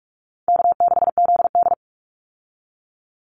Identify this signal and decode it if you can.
Morse code, heard as a keyed tone, about 44 wpm, 700 Hz. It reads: K6ZD
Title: Call Signs - 45wpm